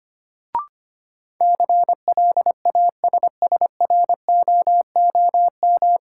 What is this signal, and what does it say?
Morse code, heard as a keyed tone, about 25 wpm, 700 Hz. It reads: CLASSROOM
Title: Sentences from Top 1500 Words - New Word Drill Alt - 25wpm